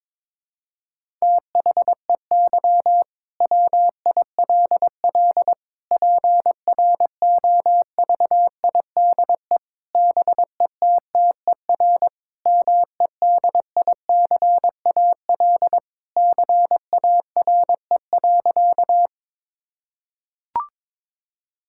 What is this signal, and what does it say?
Morse code, heard as a keyed tone, about 22 wpm, 700 Hz. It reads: THEY WILL PROVIDE BETTER MEDICAL CARE.
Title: Sentences from Top 1000 Words - Review Part 5 - 22wpm